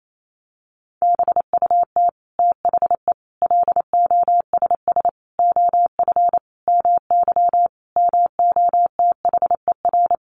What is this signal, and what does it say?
Morse code, heard as a keyed tone, about 28 wpm, 700 Hz. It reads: BUT THE LOSS OF MY MOTHER
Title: Sentences from Top 1400 Words 28wpm